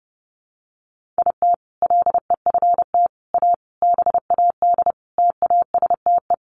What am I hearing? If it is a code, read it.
Morse code, heard as a keyed tone, about 30 wpm, 700 Hz. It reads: IT LEFT A BAD TASTE
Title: Sentences from Top 1500 Words 30wpm